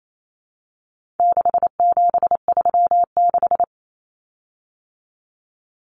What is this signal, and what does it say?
Morse code, heard as a keyed tone, about 28 wpm, 700 Hz. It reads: 6736
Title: Common Cut Numbers - 4-Digit 28wpm